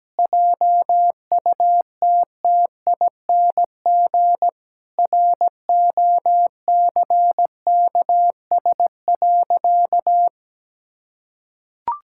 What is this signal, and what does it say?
Morse code, heard as a keyed tone, about 17 wpm, 700 Hz. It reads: JUTTING ROCKS.
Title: Pangram Challenge v2 - 17wpm